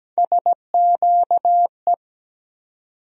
Morse code, heard as a keyed tone, about 17 wpm, 700 Hz. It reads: SQE
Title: USA Call Sign Postfixes - 3 Letters 17wpm